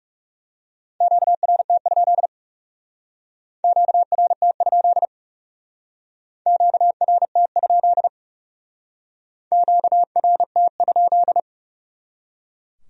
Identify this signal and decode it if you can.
Morse code, heard as a keyed tone, about 36 wpm, 700 Hz. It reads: QRT? QRT? QRT? QRT?
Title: Speed-Racing Q-Codes - 30wpm